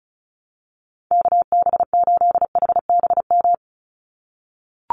Morse code, heard as a keyed tone, about 34 wpm, 700 Hz. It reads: KB8HBM
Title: USA Call Signs 35wpm